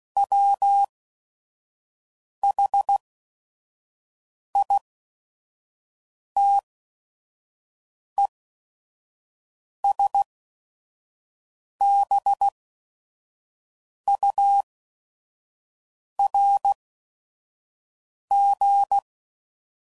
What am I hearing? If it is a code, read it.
Morse code, heard as a keyed tone, about 16 wpm, spaced as for 5 wpm, 800 Hz. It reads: WHITESBURG
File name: CW Code Groups @5WPM / fromSM's / Cities